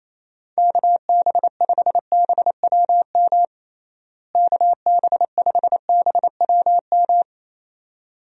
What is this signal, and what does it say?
Morse code, heard as a keyed tone, about 28 wpm, 700 Hz. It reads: KB5BWM KB5BWM
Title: USA Call Signs - 6 Characters - Repeated 2x - 28wpm